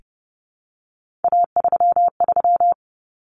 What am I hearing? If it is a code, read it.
Morse code, heard as a keyed tone, about 30 wpm, 700 Hz. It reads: A33
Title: Common Cut Numbers - 3-Digit 30wpm